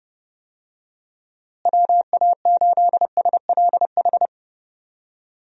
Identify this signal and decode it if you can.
Morse code, heard as a keyed tone, about 32 wpm, 700 Hz. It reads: WA8SLH
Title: USA Call Signs - 6 Characters - Repeated 3x - 30wpm